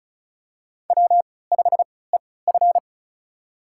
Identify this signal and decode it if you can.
Morse code, heard as a keyed tone, about 36 wpm, spaced as for 20 wpm, 700 Hz. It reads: W5EF
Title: USA Call Signs - 4 Characters - 3x Character Spacing - 35wpm